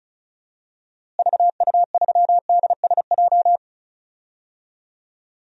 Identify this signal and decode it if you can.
Morse code, heard as a keyed tone, about 36 wpm, 700 Hz. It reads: VU3DSJ
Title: DXCC-100 Call Signs - 6 Characters - Repeated 3x - 35wpm